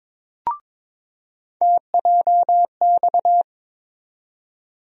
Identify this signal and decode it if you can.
Morse code, heard as a keyed tone, about 22 wpm, 700 Hz. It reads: TJX
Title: USA Call Sign Postfixes - 3 Letters 22wpm